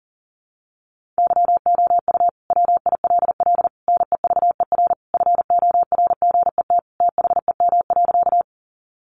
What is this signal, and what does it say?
Morse code, heard as a keyed tone, about 40 wpm, 700 Hz. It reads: YOU WILL NEVER FORGET THEM.
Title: Sentences from Top 1200 Words 40wpm